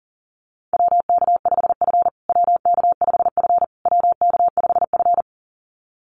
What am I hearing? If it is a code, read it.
Morse code, heard as a keyed tone, about 40 wpm, 700 Hz. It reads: WK5F WK5F WK5F
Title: USA Call Signs Repeated 3x 40wpm